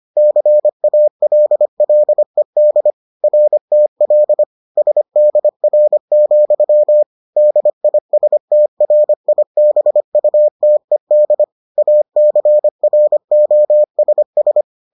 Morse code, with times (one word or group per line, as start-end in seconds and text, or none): 0.2-2.9 CALLED
3.2-4.4 RTL
4.8-7.0 SDR,
7.4-11.4 DISTRIBUTED
11.8-14.6 ACROSS